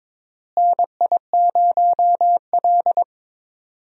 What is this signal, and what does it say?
Morse code, heard as a keyed tone, about 22 wpm, 700 Hz. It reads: NI0L
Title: USA Call Signs - 4 Characters - Repeated 2x - 22wpm